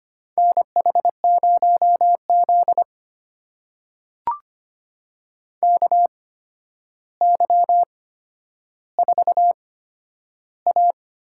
Morse code, heard as a keyed tone, about 25 wpm, spaced as for 11 wpm, 700 Hz. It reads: NH0Z KY4A
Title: USA Call Signs - 4 Characters - 8x Character Spacing - 25wpm